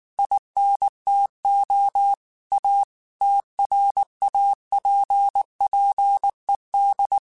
Morse code, heard as a keyed tone, about 19 wpm, 800 Hz. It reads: INTO A TRAPPED
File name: QSOs / QSOs-mp3s / WA7QOX 2@18